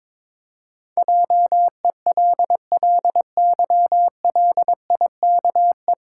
Morse code, heard as a keyed tone, about 21 wpm, 700 Hz. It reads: JELLYLIKE
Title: Pangram Challenge v2 - 22wpm